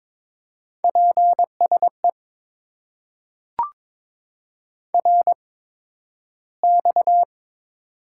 Morse code, heard as a keyed tone, about 22 wpm, spaced as for 12 wpm, 700 Hz. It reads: PSE RX
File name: Top 90 QSO Elements - 8x Character Spacing - 22wpm